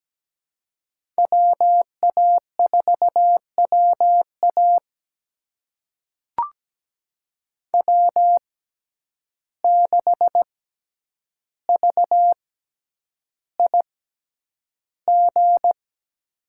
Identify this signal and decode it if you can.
Morse code, heard as a keyed tone, about 17 wpm, spaced as for 9 wpm, 700 Hz. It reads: WA4WA W6VIG
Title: USA Call Signs - 5 Characters - 6x Character Spacing - 17wpm